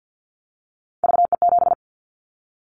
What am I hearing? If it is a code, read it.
Morse code, heard as a keyed tone, about 50 wpm, 700 Hz. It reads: VE7
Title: Callsign Prefixes for Top 100 DXCC - 50wpm